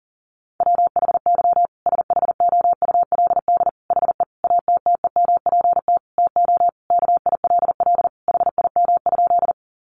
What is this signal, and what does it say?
Morse code, heard as a keyed tone, about 40 wpm, 700 Hz. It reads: WHY SHOULD HE ATTEMPT TO KILL HIM?
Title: Sentences from Top 1400 Words 40wpm